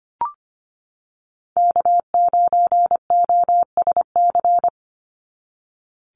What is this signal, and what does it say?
Morse code, heard as a keyed tone, about 25 wpm, 700 Hz. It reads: K9OSC
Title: NA SOTA Chaser Callsigns 25wpm